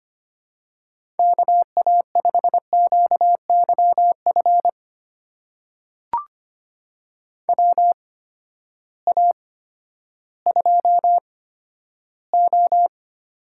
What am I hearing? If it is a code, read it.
Morse code, heard as a keyed tone, about 25 wpm, spaced as for 12 wpm, 700 Hz. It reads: KA5QYF WA2O
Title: USA Call Signs - 6 Characters - 8x Character Spacing - 25wpm